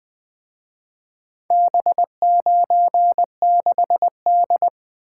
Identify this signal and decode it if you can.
Morse code, heard as a keyed tone, about 23 wpm, 700 Hz. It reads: B96D
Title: Speed-Racing - Four Letter-Number Combinations - 20wpm